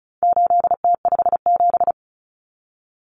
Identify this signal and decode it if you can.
Morse code, heard as a keyed tone, about 34 wpm, 700 Hz. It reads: 8T57
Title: Common Cut Numbers - 4-Digit 35wpm